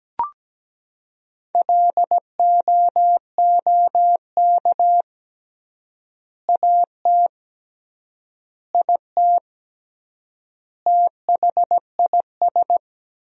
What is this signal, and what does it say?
Morse code, heard as a keyed tone, about 17 wpm, 700 Hz. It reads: LOOK AT IT THIS
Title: 3x Word-Spacing - Sentences from Top 100 Words - 17wpm